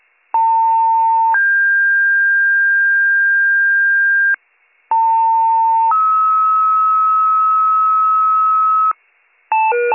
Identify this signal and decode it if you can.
Two-tone page, heard as a one-second tone, then a three-second tone, about 900 then 1600 Hz, about 900 then 1290 Hz, with noise. It is heard from 0.3 to 8.9 s.